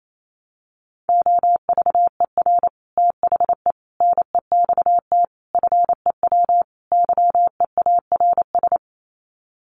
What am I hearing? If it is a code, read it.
Morse code, heard as a keyed tone, about 28 wpm, 700 Hz. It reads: OVER THE NEXT FEW YEARS
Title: Sets of 5 Words Encore 28wpm